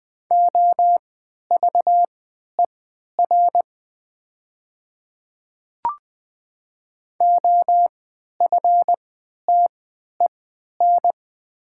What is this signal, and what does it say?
Morse code, heard as a keyed tone, about 20 wpm, spaced as for 11 wpm, 700 Hz. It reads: OVER OFTEN
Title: 3x Character Spacing - Top 500 Words - 20wpm